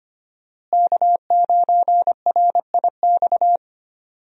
Morse code, heard as a keyed tone, about 25 wpm, 700 Hz. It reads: K9RIX
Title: USA Call Signs - 5 Characters - Repeated 2x - 25wpm